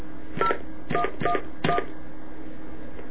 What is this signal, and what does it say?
DTMF, heard as telephone touch tones, tones of about 90 ms. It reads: 0111